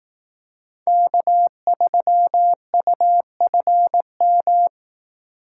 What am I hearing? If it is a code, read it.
Morse code, heard as a keyed tone, about 18 wpm, 700 Hz. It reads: K3UFM
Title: Speed Racing - USA Call Signs 15wpm